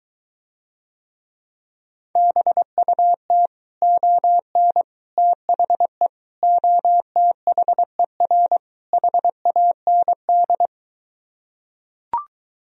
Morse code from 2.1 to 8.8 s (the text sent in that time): BUT ON THE OTHER